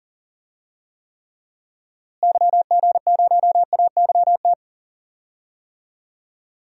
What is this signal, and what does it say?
Morse code, heard as a keyed tone, about 40 wpm, 700 Hz. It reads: YG0AYT
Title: DXCC-100 Call Signs - 6 Characters - Repeated 3x - 40wpm